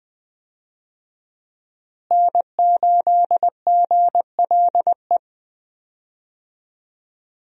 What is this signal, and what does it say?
Morse code, heard as a keyed tone, about 20 wpm, 700 Hz. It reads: N8GLE
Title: USA Call Signs - 5 Characters - Repeated 2x - 20wpm